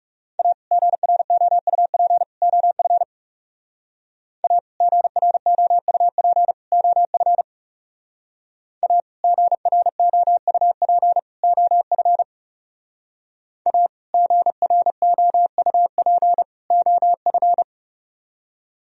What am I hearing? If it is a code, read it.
Morse code, heard as a keyed tone, about 36 wpm, 700 Hz. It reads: A GROUP OF A GROUP OF A GROUP OF A GROUP OF
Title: Speed-Racing - Sets of 3 Words 30wpm